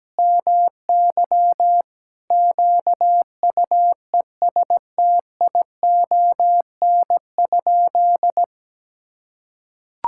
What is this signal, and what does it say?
Morse code, heard as a keyed tone, about 17 wpm, 700 Hz. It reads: MY QUESTION?
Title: Sentences from Top 1000 Words - Review Part 5 - 17wpm